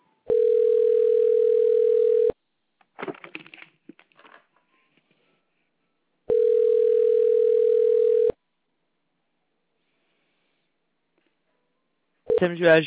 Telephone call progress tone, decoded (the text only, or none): ringback tone